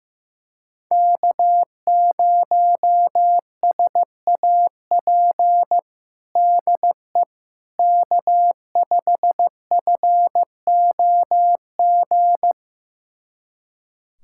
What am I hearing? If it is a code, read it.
Morse code, heard as a keyed tone, about 15 wpm, 700 Hz. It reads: K0SAP DE K5FOG
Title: USA Callsign DE Callsign - 5 Characters - 15wpm